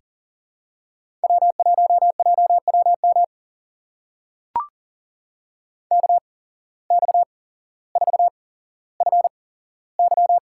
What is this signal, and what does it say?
Morse code, heard as a keyed tone, about 40 wpm, spaced as for 17 wpm, 700 Hz. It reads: W1JWM KX4FY
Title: USA Call Signs - 5 Characters - 8x Character Spacing - 40wpm